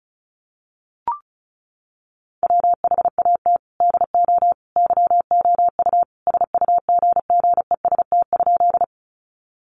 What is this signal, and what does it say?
Morse code, heard as a keyed tone, about 34 wpm, 700 Hz. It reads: WHAT DO YOU SUGGEST?
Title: Sentences from Top 1300 Words 35wpm